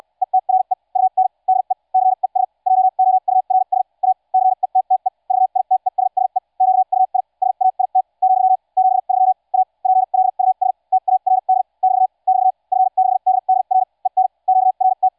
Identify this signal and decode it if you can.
Morse code, heard as a keyed tone, about 74 wpm, spaced as for 52 wpm, 750 Hz. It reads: ETRE UN DE CES JOURS A CETTE VITESSE QUI SAIT , 73 A PLUS TARD <BT> <BT> F6DDR EN QRT